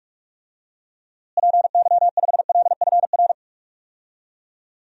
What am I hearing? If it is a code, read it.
Morse code, heard as a keyed tone, about 44 wpm, 700 Hz. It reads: PY5LFR
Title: DXCC-100 Call Signs - 6 Characters - Repeated 2x - 45wpm